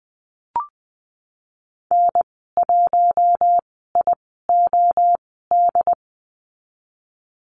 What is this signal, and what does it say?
Morse code, heard as a keyed tone, about 20 wpm, spaced as for 15 wpm, 700 Hz. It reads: N1IOD
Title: 069 Lesson 12g - Letter d - Call Signs 20wpm